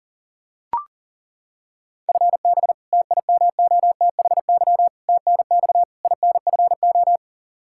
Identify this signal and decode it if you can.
Morse code, heard as a keyed tone, about 40 wpm, 700 Hz. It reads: FB TIMOTHY TNX INFO
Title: K1USN SST QSO Practice 40wpm